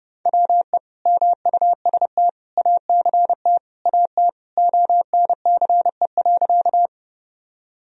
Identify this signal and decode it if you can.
Morse code, heard as a keyed tone, about 32 wpm, 700 Hz. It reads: WE MUST ACT AT ONCE.
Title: Sentences from Top 1000 Words - Review Part 1 - 30wpm